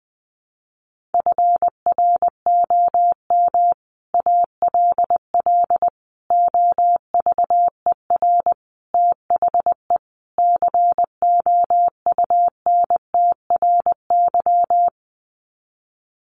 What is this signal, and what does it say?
Morse code, heard as a keyed tone, about 20 wpm, 700 Hz. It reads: FROM ALL OVER THE COUNTRY
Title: Sets of 5 Words Encore 20wpm